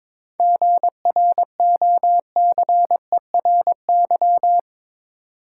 Morse code, heard as a keyed tone, about 22 wpm, 700 Hz. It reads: GROCERY
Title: Seven Letter Words - 22wpm